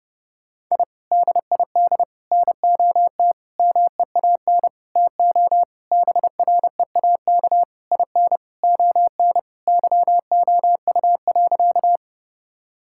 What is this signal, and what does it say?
Morse code, heard as a keyed tone, about 30 wpm, 700 Hz. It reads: I DID NOT MEAN TO BREAK IN ON YOU.
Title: Sentences from Top 1000 Words - Review Part 6 - 30wpm